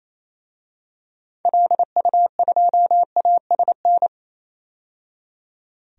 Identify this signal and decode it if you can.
Morse code, heard as a keyed tone, about 28 wpm, 700 Hz. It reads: LU2ASN
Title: DXCC-100 Call Signs - 6 Characters - 28wpm